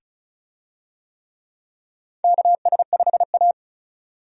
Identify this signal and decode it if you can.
Morse code, heard as a keyed tone, about 36 wpm, 700 Hz. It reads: KS5A
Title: USA Call Signs - 4 Characters - 35wpm